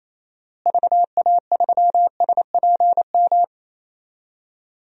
Morse code, heard as a keyed tone, about 28 wpm, 700 Hz. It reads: VA3SPM
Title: DXCC-100 Call Signs - 6 Characters - Repeated 3x - 28wpm